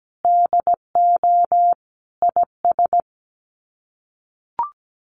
Morse code, heard as a keyed tone, about 17 wpm, 700 Hz. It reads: DO IS
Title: Sets of 5 Words Encore2 17wpm